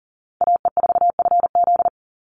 Morse code, heard as a keyed tone, about 40 wpm, 700 Hz. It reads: AE4FZ
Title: NA SOTA Chaser Callsigns 40wpm